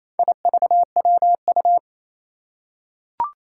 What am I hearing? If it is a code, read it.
Morse code, heard as a keyed tone, about 28 wpm, 700 Hz. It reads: IVWU
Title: Four Challenging Letter-Number Combinations 28wpm